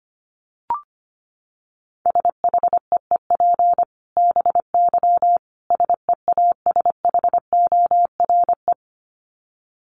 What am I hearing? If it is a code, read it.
Morse code, heard as a keyed tone, about 25 wpm, 700 Hz. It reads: SHEEP BY SEASHORE
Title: Dit Challenge 25wpm